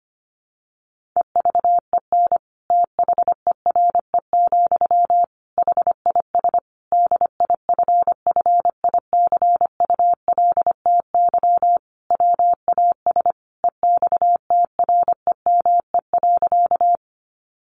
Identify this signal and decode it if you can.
Morse code, heard as a keyed tone, about 25 wpm, 700 Hz. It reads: EVEN THERE, HIS DIFFICULTY WAS EXTREME.
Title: Sentences from Top 1500 Words 25wpm